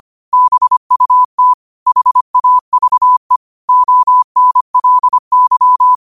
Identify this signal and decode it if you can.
Morse code, heard as a keyed tone, about 25 wpm, 1000 Hz. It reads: BUT HAVE ONLY
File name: qotdfiles / 25wpm